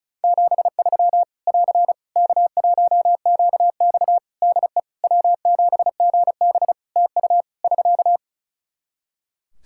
Morse code, heard as a keyed tone, about 36 wpm, 700 Hz. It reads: 73 <AR> K1QX DE W7GB TU <SK>
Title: K1USN SST QSO Practice 35wpm